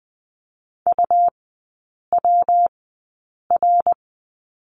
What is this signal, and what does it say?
Morse code, heard as a keyed tone, about 20 wpm, spaced as for 8 wpm, 700 Hz. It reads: UWR
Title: 114 Lesson 19e - Letter w - Three Characters 20wpm